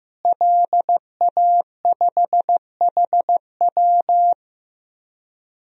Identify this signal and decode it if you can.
Morse code, heard as a keyed tone, about 15 wpm, 700 Hz. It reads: LA5HW
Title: DXCC-100 Call Signs - 5 Characters - Repeated 3x - 15wpm